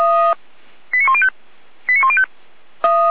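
Five-tone selective call (ZVEI), heard as quick runs of five tones, tones of about 70 ms. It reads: 89105 89105
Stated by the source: ZVEI five-tone selective call received by SDR.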